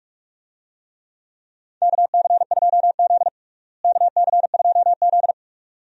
Morse code, heard as a keyed tone, about 46 wpm, 700 Hz. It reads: KC2Z KC2Z
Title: USA Call Signs - 4 Characters - Repeated 2x - 45wpm